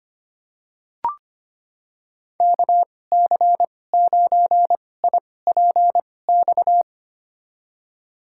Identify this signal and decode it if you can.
Morse code, heard as a keyed tone, about 25 wpm, spaced as for 18 wpm, 700 Hz. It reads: KC9IPX
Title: USA Call Signs - 6 Characters - 2x Character Spacing - 25wpm